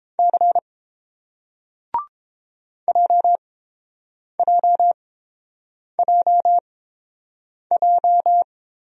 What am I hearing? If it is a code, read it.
Morse code, heard as a keyed tone, about 30 wpm, spaced as for 7 wpm, 700 Hz. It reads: C JJJJ